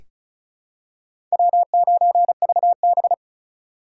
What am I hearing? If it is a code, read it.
Morse code, heard as a keyed tone, about 36 wpm, 700 Hz. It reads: W9VB
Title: USA Call Signs - 4 Characters - 8x Character Spacing - 35wpm